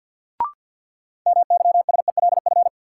Morse code, heard as a keyed tone, about 51 wpm, 700 Hz. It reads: MYSELF
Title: Sentences from Top 1300 Words - New Word Drill Alt - 50wpm